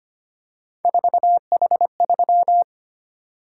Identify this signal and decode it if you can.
Morse code, heard as a keyed tone, about 25 wpm, 700 Hz. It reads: 4H3
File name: Three Letter-Number Combinations - 25wpm